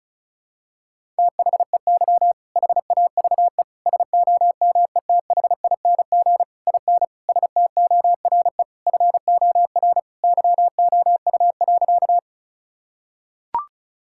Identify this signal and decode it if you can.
Morse code, heard as a keyed tone, about 36 wpm, 700 Hz. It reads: THEY HAVE SOMETHING IN STORE FOR YOU.
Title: Sentences from Top 1000 Words - Review Part 5 - 35wpm